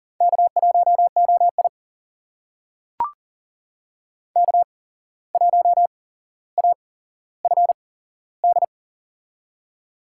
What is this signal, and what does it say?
Morse code, heard as a keyed tone, about 42 wpm, spaced as for 16 wpm, 700 Hz. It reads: K1OI K1AFD
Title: USA Call Signs - 8x Character Spacing - 40wpm